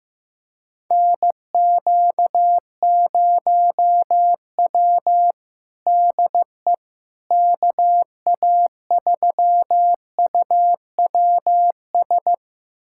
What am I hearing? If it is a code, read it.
Morse code, heard as a keyed tone, about 15 wpm, 700 Hz. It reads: NQ0W DE KA3UWS
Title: USA Callsign DE Callsign 15wpm